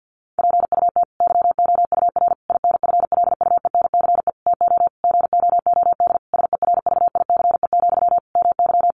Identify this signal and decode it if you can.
Morse code, heard as a keyed tone, about 48 wpm, 700 Hz. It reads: PUT YOUR INFLUENCE TO GOOD SERVICE, MY
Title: Sentences from Top 1100 Words 50wpm